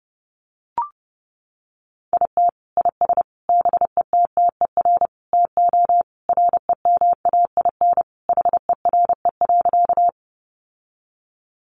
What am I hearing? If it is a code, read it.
Morse code, heard as a keyed tone, about 30 wpm, 700 Hz. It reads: IT IS BETTER TO REMAIN HERE.
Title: Sentences from Top 1200 Words 30wpm